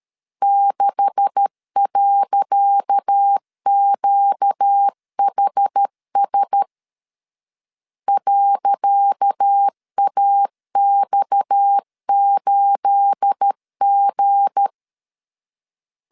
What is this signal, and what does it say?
Morse code, heard as a keyed tone, about 13 wpm, 800 Hz. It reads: 6.QHS .AX8G
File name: LEZIONI / 10 SIGNS / 10C DOT 12wpm 2451